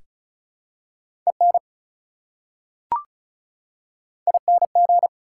Morse code, heard as a keyed tone, about 36 wpm, 700 Hz. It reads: EN ING
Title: Prefixes and Suffixes 35wpm